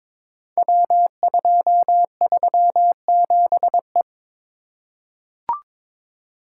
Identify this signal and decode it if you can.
Morse code, heard as a keyed tone, about 23 wpm, 700 Hz. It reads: W237E
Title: Code Groups - 5 Letter:Numbers - 22wpm